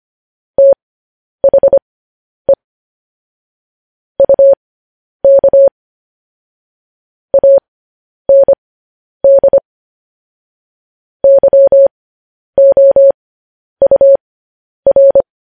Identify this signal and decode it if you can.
Morse code, heard as a keyed tone, about 25 wpm, spaced as for 10 wpm, 550 Hz. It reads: THE UK AND YOUR
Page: news headlines in morse code at 10 wpm